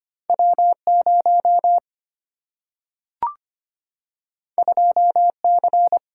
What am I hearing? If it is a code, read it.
Morse code, heard as a keyed tone, about 25 wpm, 700 Hz. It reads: W0 2C
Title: Single-Letter Single-Number 25wpm